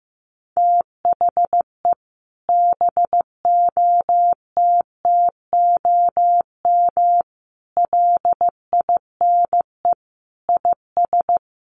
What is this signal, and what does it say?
Morse code, heard as a keyed tone, about 15 wpm, 700 Hz. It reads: THE BOTTOM LINE IS
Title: Sets of 5 Words Encore 15wpm